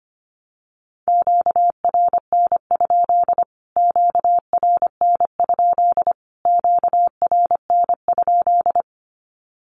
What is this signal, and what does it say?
Morse code, heard as a keyed tone, about 25 wpm, 700 Hz. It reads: QRN? QRN? QRN?
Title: Q-Codes Repeated 3x 25wpm